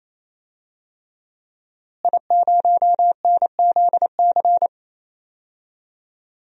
Morse code, heard as a keyed tone, about 27 wpm, 700 Hz. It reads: I0NZC
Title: DXCC-100 Call Signs - 5 Characters - Repeated 3x - 28wpm